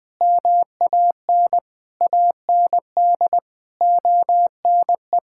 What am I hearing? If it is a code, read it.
Morse code, heard as a keyed tone, about 19 wpm, 700 Hz. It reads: MAN AND ONE
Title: Sets of 6 Words - 20wpm